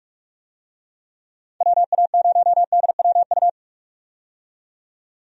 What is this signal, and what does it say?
Morse code, heard as a keyed tone, about 46 wpm, 700 Hz. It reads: WA0DWU
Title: USA Call Signs - 6 Characters - 45wpm